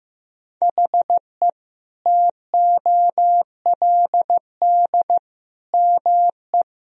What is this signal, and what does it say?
Morse code, heard as a keyed tone, about 15 wpm, 700 Hz. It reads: HE TOLD ME
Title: Sentences from Top 1000 Words - Review Part 2 - 15wpm